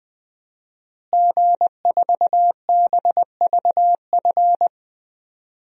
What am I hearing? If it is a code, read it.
Morse code, heard as a keyed tone, about 20 wpm, 700 Hz. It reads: G4BVF
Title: DXCC-100 Call Signs - 5 Characters - Repeated 3x - 20wpm